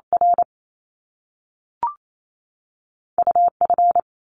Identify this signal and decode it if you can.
Morse code, heard as a keyed tone, about 28 wpm, spaced as for 3 wpm, 700 Hz. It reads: RU F
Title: Top Two Letter Combinations in Words 28wpm